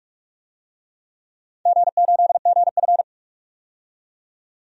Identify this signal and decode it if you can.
Morse code, heard as a keyed tone, about 46 wpm, 700 Hz. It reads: G8GF
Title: DXCC-100 Call Signs - 4 Characters - 45wpm